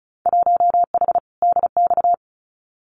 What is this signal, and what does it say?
Morse code, heard as a keyed tone, about 36 wpm, 700 Hz. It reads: 1H DX
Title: Winter Field Day Exchanges 35wpm